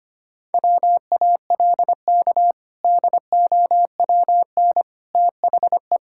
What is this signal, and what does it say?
Morse code, heard as a keyed tone, about 25 wpm, 700 Hz. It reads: WALK DOWN THE
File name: practice / 25wpm / Sentences with 2-Word Phrases - 25wpm